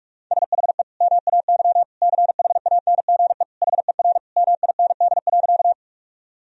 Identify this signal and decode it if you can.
Morse code, heard as a keyed tone, about 46 wpm, 700 Hz. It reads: SHE MAY CHANGE HER MIND.
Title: Sentences from Top 1000 Words - Review Part 1 - 45wpm